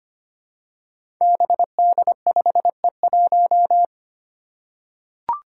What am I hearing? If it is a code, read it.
Morse code, heard as a keyed tone, about 25 wpm, 700 Hz. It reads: BD5E1
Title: Five Challenging Letter-Number Combinations 25wpm